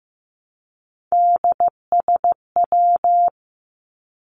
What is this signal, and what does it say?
Morse code, heard as a keyed tone, about 15 wpm, 700 Hz. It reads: DSW
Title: Abbreviations 15wpm